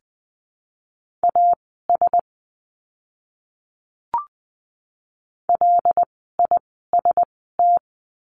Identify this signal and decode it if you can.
Morse code, heard as a keyed tone, about 20 wpm, spaced as for 14 wpm, 700 Hz. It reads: AS LIST
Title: 075 Lesson 13f - Letter L - Words 20wpm